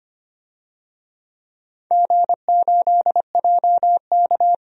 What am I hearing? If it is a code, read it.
Morse code, heard as a keyed tone, about 25 wpm, 700 Hz. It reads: G8JK
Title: DXCC-100 Call Signs - 4 Characters - 25wpm